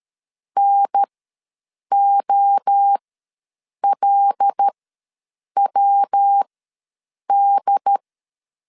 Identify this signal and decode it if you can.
Morse code, heard as a keyed tone, about 13 wpm, spaced as for 7 wpm, 800 Hz. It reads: NOLWD
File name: LEZIONI / 5 LBVC / 5 L ws 12wpm 2958